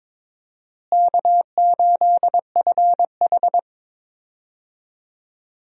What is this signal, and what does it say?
Morse code, heard as a keyed tone, about 22 wpm, 700 Hz. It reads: K8FH
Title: USA Call Signs - 4 Characters - Repeated 3x - 22wpm